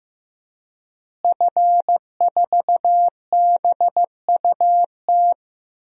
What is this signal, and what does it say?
Morse code, heard as a keyed tone, about 15 wpm, 700 Hz. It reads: F4BUT
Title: DXCC-100 Call Signs - 5 Characters - Repeated 3x - 15wpm